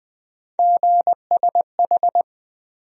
Morse code, heard as a keyed tone, about 20 wpm, 700 Hz. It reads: GSH
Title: USA Call Sign Postfixes - 3 Letters 20wpm